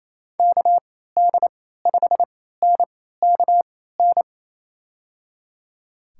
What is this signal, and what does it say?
Morse code, heard as a keyed tone, about 27 wpm, spaced as for 16 wpm, 700 Hz. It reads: KD5NKN
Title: USA Call Signs - 3x Character Spacing - 28wpm